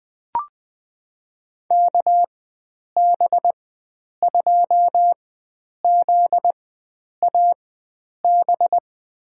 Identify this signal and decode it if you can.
Morse code, heard as a keyed tone, about 20 wpm, spaced as for 9 wpm, 700 Hz. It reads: KB2ZAB
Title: USA Call Signs - 6 Characters - 4x Character Spacing - 20wpm